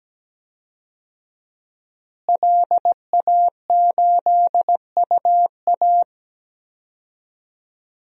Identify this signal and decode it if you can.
Morse code, heard as a keyed tone, about 17 wpm, 700 Hz. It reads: LA8UA
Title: DXCC-100 Call Signs - 5 Characters - Repeated 3x - 17wpm